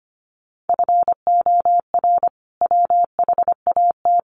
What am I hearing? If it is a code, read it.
Morse code, heard as a keyed tone, about 25 wpm, 700 Hz. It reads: FOR WHAT
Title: Sentences from Top 700 Words 25wpm